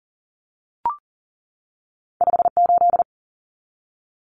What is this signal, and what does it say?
Morse code, heard as a keyed tone, about 40 wpm, 700 Hz. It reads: 58
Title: Numbers - 2-Digit 40wpm